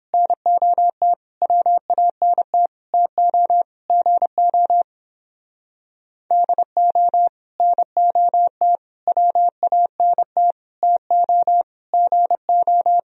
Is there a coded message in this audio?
Yes: Morse code from 0.1 to 13.0 s, heard as a keyed tone, about 31 wpm, 700 Hz. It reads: NOT WANT TO GO DO NOT WANT TO GO